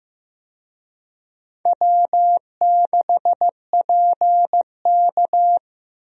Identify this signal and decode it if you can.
Morse code, heard as a keyed tone, about 16 wpm, 700 Hz. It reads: W6PK
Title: USA Call Signs - 4 Characters - Repeated 2x - 15wpm